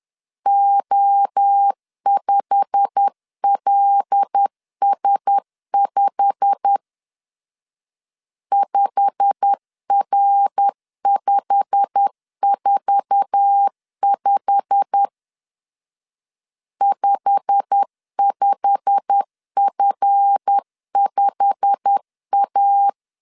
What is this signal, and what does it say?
Morse code, heard as a keyed tone, about 10 wpm, 800 Hz. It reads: O5LS5 5R545 55F5A
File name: LEZIONI / 8 YJ45 / 8 YJ45 10wpm 2549